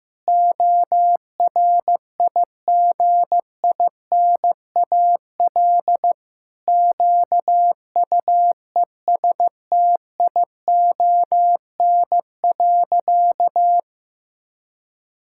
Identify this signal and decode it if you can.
Morse code, heard as a keyed tone, about 16 wpm, 700 Hz. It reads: ORIGINAL QUESTION.
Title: Sentences from Top 1000 Words - Review Part 6 - 15wpm